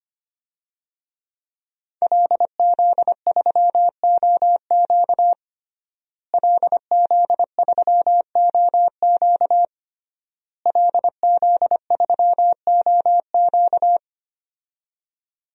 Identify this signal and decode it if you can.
Morse code, heard as a keyed tone, about 25 wpm, 700 Hz. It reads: LZ3OQ LZ3OQ LZ3OQ
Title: DXCC-100 Call Signs - 5 Characters - Repeated 3x - 25wpm